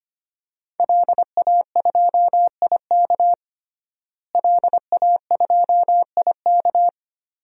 Morse code, heard as a keyed tone, about 25 wpm, 700 Hz. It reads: LA2IK LA2IK
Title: DXCC-100 Call Signs - 5 Characters - Repeated 2x - 25wpm